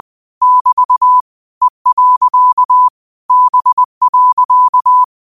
Morse code, heard as a keyed tone, about 20 wpm, 1000 Hz. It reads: <BT> E. B.
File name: qotdfiles / 20wpm